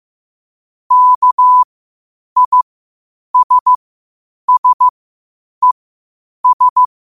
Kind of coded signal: Morse code